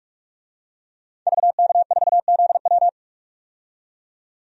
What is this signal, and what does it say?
Morse code, heard as a keyed tone, about 46 wpm, 700 Hz. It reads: VK4ZW
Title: DXCC-100 Call Signs - 5 Characters - 45wpm